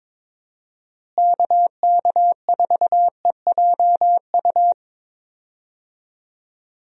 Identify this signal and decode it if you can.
Morse code, heard as a keyed tone, about 22 wpm, 700 Hz. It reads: KK4EJU
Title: USA Call Signs - 6 Characters - Repeated 3x - 22wpm